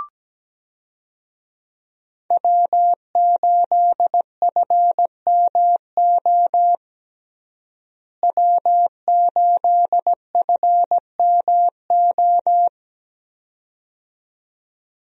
Morse code, heard as a keyed tone, about 17 wpm, 700 Hz. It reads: W8FMO W8FMO
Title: USA Call Signs - 5 Characters - Repeated 2x - 17wpm